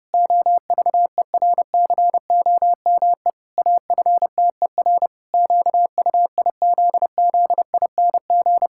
Morse code, heard as a keyed tone, about 30 wpm, 700 Hz. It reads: OVERCOME AFTER QUIZZING
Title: Pangram Challenge v2 - 30wpm